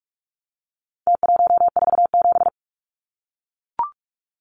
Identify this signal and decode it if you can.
Morse code, heard as a keyed tone, about 44 wpm, 700 Hz. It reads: T147
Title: Numbers - Contest Style - 4-Digit 45wpm